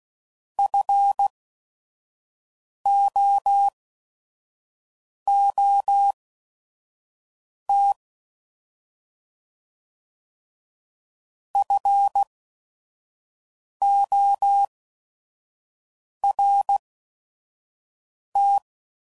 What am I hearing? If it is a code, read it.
Morse code, heard as a keyed tone, about 16 wpm, spaced as for 5 wpm, 800 Hz. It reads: FOOT FORT
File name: Prefixes / By First Letter / Fs